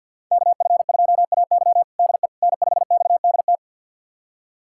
Morse code, heard as a keyed tone, about 47 wpm, 700 Hz. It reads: KF2AY DE N5XDT
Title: USA Callsign DE Callsign - 5 Characters - 50wpm